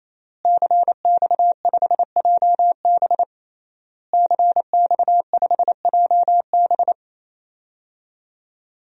Morse code, heard as a keyed tone, about 28 wpm, 700 Hz. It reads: CX5JB CX5JB
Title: DXCC-100 Call Signs - 5 Characters - Repeated 3x - 28wpm